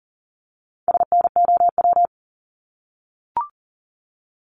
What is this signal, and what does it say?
Morse code, heard as a keyed tone, about 38 wpm, 700 Hz. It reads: SNOW
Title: Multi-letter Phonemes 40wpm